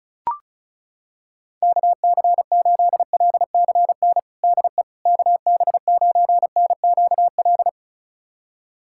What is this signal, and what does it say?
Morse code, heard as a keyed tone, about 36 wpm, 700 Hz. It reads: KC8LCN DE KB9NQL
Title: USA Callsign DE Callsign - 6 Characters - 35wpm